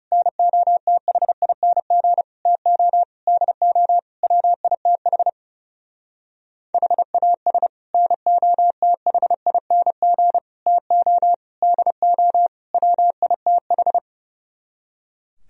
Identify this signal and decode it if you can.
Morse code, heard as a keyed tone, about 34 wpm, 700 Hz. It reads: NOTHING TO DO WITH HAS NOTHING TO DO WITH
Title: Speed-Racing - Sets of 5 Words 30wpm